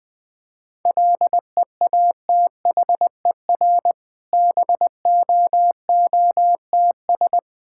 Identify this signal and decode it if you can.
Morse code, heard as a keyed tone, about 20 wpm, 690 Hz. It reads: LEATHER BOOTS
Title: Difficult Words with Hints - 20wpm